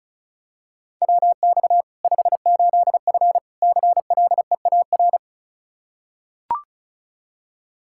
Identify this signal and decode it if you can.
Morse code, heard as a keyed tone, about 36 wpm, 700 Hz. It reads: WX 58F CLEAR
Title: Practice QSO - Exchange 2 - Info - 35wpm